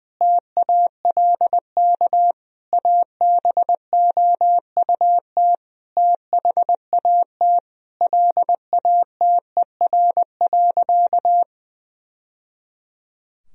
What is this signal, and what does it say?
Morse code, heard as a keyed tone, about 20 wpm, 700 Hz. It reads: TALK ABOUT THAT LATER.
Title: Sentences from Top 1000 Words - Review Part 4 - 20wpm